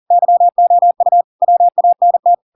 Morse code, heard as a keyed tone, about 40 wpm, 700 Hz. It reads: YOU WANT